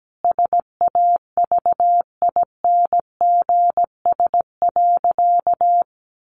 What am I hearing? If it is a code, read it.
Morse code, heard as a keyed tone, about 17 wpm, 700 Hz. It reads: SAVINGS.
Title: Sentences from Top 1400 Words 17wpm